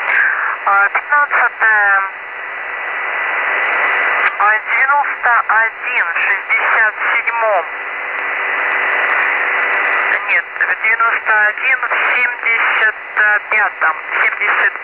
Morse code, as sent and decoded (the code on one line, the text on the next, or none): none